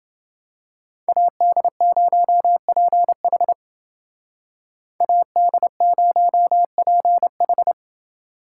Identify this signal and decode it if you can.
Morse code, heard as a keyed tone, about 29 wpm, 700 Hz. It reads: AD0PH AD0PH